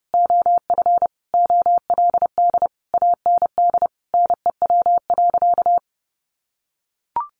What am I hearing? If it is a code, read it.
Morse code, heard as a keyed tone, about 30 wpm, 700 Hz. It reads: OF OLD AND NEW.
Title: Sentences from Top 1500 Words 30wpm